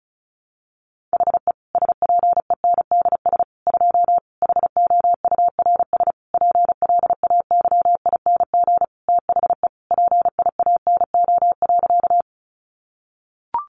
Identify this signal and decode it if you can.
Morse code, heard as a keyed tone, about 34 wpm, 700 Hz. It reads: HE SPENDS 2 HOURS PLAYING THE PIANO.
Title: Additional Easy Phrases 35wpm